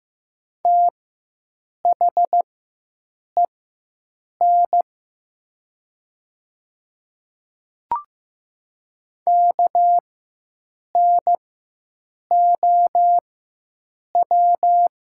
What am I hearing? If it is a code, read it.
Morse code, heard as a keyed tone, about 15 wpm, spaced as for 7 wpm, 700 Hz. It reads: THEN KNOW
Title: 4x Character Spacing - Top 200 Words - 15wpm